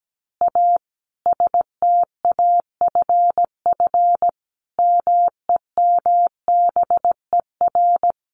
Morse code, heard as a keyed tone, about 17 wpm, 700 Hz. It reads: A STAFF MEMBER
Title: Additional Easy Phrases 17wpm